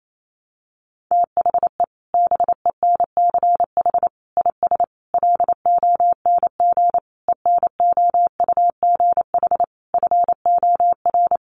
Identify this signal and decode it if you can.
Morse code, heard as a keyed tone, about 28 wpm, 700 Hz. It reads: THE BENCH IS LONG ENOUGH FOR
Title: Sentences from Top 1400 Words 28wpm